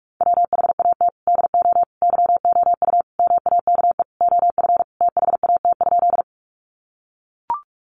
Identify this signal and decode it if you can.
Morse code, heard as a keyed tone, about 44 wpm, 700 Hz. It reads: WHAT DO YOU MAKE OF THAT?
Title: Sentences from Top 100 Words 45wpm